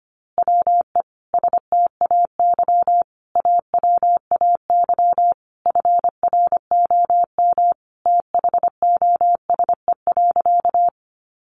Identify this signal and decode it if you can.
Morse code, heard as a keyed tone, about 25 wpm, 700 Hz. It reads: WE STAY AWAY FROM THOSE.
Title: A Sound Of Thunder 25wpm